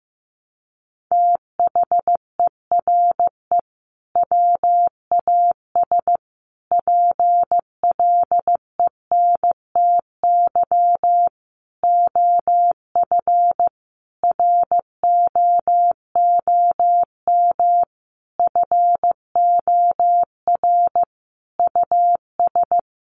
Morse code, heard as a keyed tone, about 15 wpm, 700 Hz. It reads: THERE WAS PLENTY OF ROOM FOR US
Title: Sentences from Top 1300 Words 15wpm